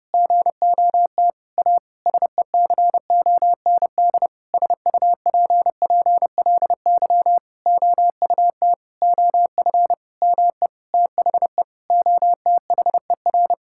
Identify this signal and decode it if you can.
Morse code, heard as a keyed tone, about 28 wpm, 700 Hz. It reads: GOT A SECOND SUPPLY OUT OF ME THE OTHER
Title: Sentences from Top 1000 Words - Review Part 5 - 30wpm